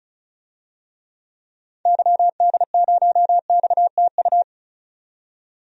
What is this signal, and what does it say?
Morse code, heard as a keyed tone, about 34 wpm, 700 Hz. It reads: YD0XTU
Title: DXCC-100 Call Signs - 6 Characters - 35wpm